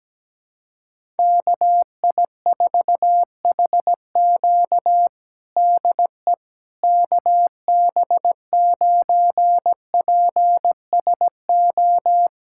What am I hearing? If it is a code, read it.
Morse code, heard as a keyed tone, about 17 wpm, 700 Hz. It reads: KI4HQ DE KB9PSO
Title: USA Callsign DE Callsign 17wpm